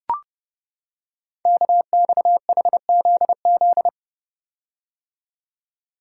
Morse code, heard as a keyed tone, about 30 wpm, 700 Hz. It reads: KXHZZ
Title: Code Groups - 5 Letters - 30wpm